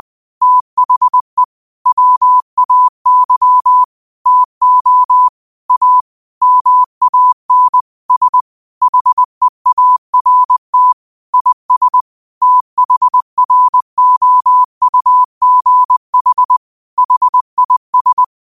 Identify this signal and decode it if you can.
Morse code, heard as a keyed tone, about 20 wpm, 1000 Hz. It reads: THE WAY TO A MAN S HEART IS THROUGH HIS
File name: qotdfiles / 20wpm